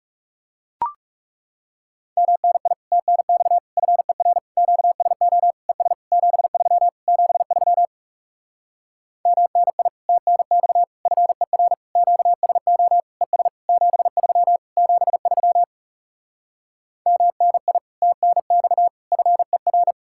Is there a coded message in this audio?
Yes: Morse code from 2.2 to 19.9 s, heard as a keyed tone, about 40 wpm, 700 Hz. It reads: MNI TNX FER QSO ES 73 73 MNI TNX FER QSO ES 73 73 MNI TNX FER